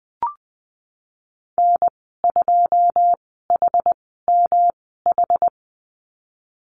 Morse code, heard as a keyed tone, about 20 wpm, spaced as for 14 wpm, 700 Hz. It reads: N2HMH
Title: 154 Lesson 25g - Letter p - Call Signs 20wpm